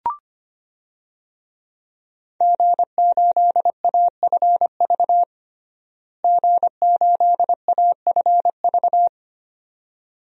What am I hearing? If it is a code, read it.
Morse code, heard as a keyed tone, about 25 wpm, 700 Hz. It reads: G8AFV G8AFV
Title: DXCC-100 Call Signs - 5 Characters - Repeated 2x - 25wpm